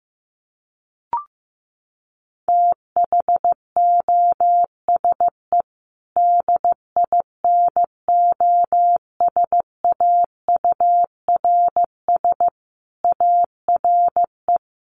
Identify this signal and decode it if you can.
Morse code, heard as a keyed tone, about 15 wpm, 700 Hz. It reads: THOSE DINOSAURS ARE
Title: A Sound Of Thunder 15wpm